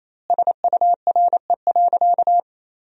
Morse code, heard as a keyed tone, about 28 wpm, 700 Hz. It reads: SURE.
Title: 2x Word-Spacing - Sentences from Top 500 Words - 28wpm